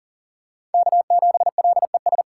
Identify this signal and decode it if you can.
Morse code, heard as a keyed tone, about 38 wpm, 700 Hz. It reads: K7LES
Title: USA Call Signs - 5 Characters - 4x Character Spacing - 40wpm